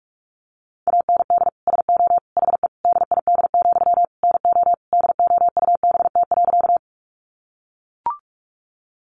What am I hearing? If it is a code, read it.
Morse code, heard as a keyed tone, about 44 wpm, 700 Hz. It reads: AND SO HE DID, NO DOUBT.
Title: Sentences from Top 1500 Words 45wpm